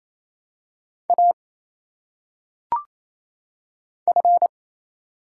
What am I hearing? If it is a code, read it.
Morse code, heard as a keyed tone, about 27 wpm, 700 Hz. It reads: A F
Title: Single Letters - Phonetic Pronunciation 28wpm